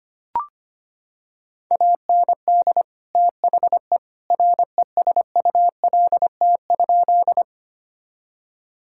Morse code, heard as a keyed tone, about 25 wpm, 700 Hz. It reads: AND THE RESULT?
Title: Sentences from Top 1000 Words - Review Part 5 - 25wpm